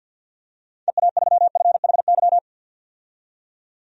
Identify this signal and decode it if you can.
Morse code, heard as a keyed tone, about 52 wpm, 700 Hz. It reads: EA3FHY
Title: DXCC-100 Call Signs - 6 Characters - 50wpm